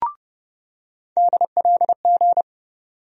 Morse code, heard as a keyed tone, about 31 wpm, 700 Hz. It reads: DLG
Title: USA Call Sign Postfixes - 3 Letters 30wpm